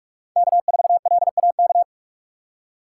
Morse code, heard as a keyed tone, about 41 wpm, 700 Hz. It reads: K4LAK
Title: USA Call Signs - 5 Characters - 2x Character Spacing - 45wpm